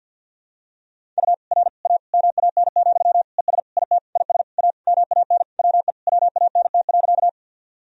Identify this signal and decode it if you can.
Morse code, heard as a keyed tone, about 51 wpm, 700 Hz. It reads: U R A MAN, ES IT IS A MAN WE WANT.
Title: Sentences from Top 200 Words and Abbreviations 50wpm